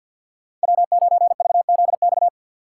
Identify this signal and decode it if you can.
Morse code, heard as a keyed tone, about 51 wpm, 690 Hz. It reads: W9VZX
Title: USA Call Signs - 4x Character Spacing - 50wpm